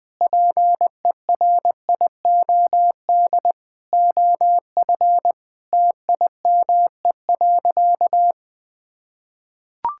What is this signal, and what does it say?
Morse code, heard as a keyed tone, about 19 wpm, 700 Hz. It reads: PERIOD OF TIME.
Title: Sentences from Top 1000 Words - Review Part 5 - 20wpm